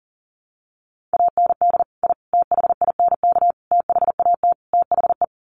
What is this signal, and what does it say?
Morse code, heard as a keyed tone, about 37 wpm, 700 Hz. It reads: AND I THINK THAT THE
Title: Sets of 5 Words 40wpm